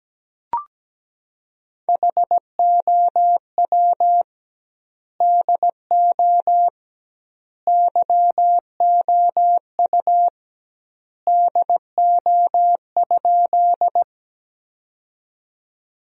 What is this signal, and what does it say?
Morse code, heard as a keyed tone, about 17 wpm, 700 Hz. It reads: HOW DO YOU DO?
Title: 2x Word-Spacing - Sentences from Top 100 Words - 17wpm